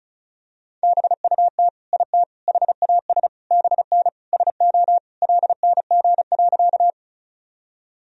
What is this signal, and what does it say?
Morse code, heard as a keyed tone, about 34 wpm, 700 Hz. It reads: BUT IT HAS BN SO LNG.
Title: Sentences from Top 100 Words and Abbreviations 35wpm